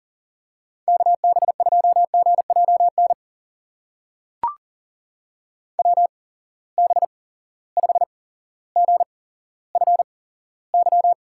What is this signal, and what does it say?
Morse code, heard as a keyed tone, about 40 wpm, spaced as for 17 wpm, 700 Hz. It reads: KB2GJN WB5GFY
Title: USA Call Signs - 6 Characters - 8x Character Spacing - 40wpm